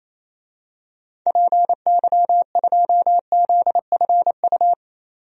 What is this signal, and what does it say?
Morse code, heard as a keyed tone, about 28 wpm, 700 Hz. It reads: PY2ZFU
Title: DXCC-100 Call Signs - 6 Characters - 28wpm